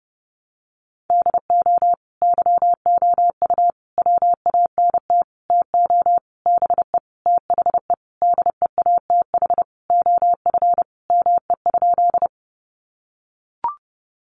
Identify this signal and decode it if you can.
Morse code, heard as a keyed tone, about 30 wpm, 700 Hz. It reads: DO YOU WANT TO BE THE DEATH OF ME?
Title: Sentences from Top 1100 Words 30wpm